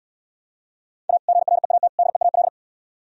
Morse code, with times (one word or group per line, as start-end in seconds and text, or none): 1.1-1.2 A
1.3-1.9 GREAT
2.0-2.5 DEAL